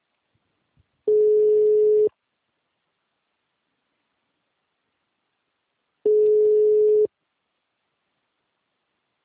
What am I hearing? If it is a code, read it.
Ringback tone (Europe, 425 Hz).
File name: out- -2002